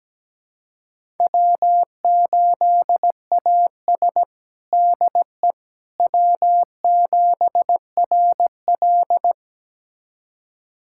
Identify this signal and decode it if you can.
Morse code, heard as a keyed tone, about 17 wpm, 700 Hz. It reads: W8AS DE W7RL
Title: USA Callsign DE Callsign - 4 Characters - 17wpm